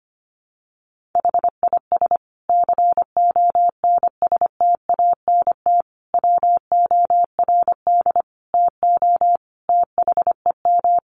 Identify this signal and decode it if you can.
Morse code, heard as a keyed tone, about 25 wpm, 700 Hz. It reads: HIS CONSTANT WORD TO THEM
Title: Sentences from Top 1500 Words 25wpm